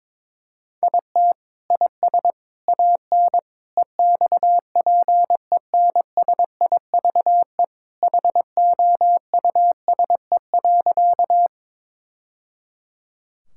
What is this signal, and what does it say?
Morse code, heard as a keyed tone, about 22 wpm, 700 Hz. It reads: IT IS AN EXPENSIVE HOUSE.
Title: Sentences from Top 1000 Words - Review Part 6 - 22wpm